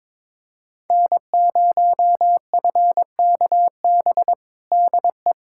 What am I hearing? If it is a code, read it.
Morse code, heard as a keyed tone, about 22 wpm, 700 Hz. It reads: N0FKB DE
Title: USA Callsign DE Callsign - 5 Characters - 22wpm